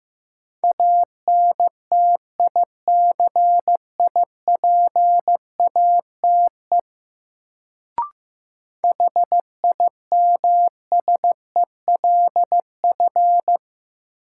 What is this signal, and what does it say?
Morse code, heard as a keyed tone, about 15 wpm, 700 Hz. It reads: ANTICIPATE HIMSELF
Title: Sentences from Top 1500 Words - New Word Drill Alt - 15wpm